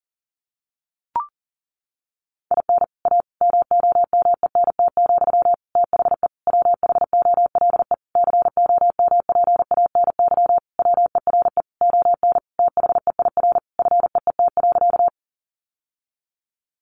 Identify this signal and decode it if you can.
Morse code, heard as a keyed tone, about 40 wpm, 700 Hz. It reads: IN A MOMENT, THE WHOLE COMPANY WERE ON THEIR FEET.
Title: Sentences from Top 1000 Words 40wpm